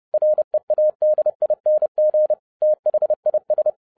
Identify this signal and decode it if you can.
Morse code, heard as a keyed tone, about 30 wpm, 610 Hz. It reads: READING THIS
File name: 2012-01-mp3 / n12-I DIDNT MISS THE SOLAR CYCLE 24-30wpm